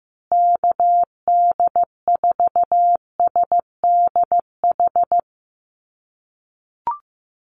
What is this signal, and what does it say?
Morse code, heard as a keyed tone, about 15 wpm, 700 Hz. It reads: KD4SDH
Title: USA Call Signs Repeated 2x 15wpm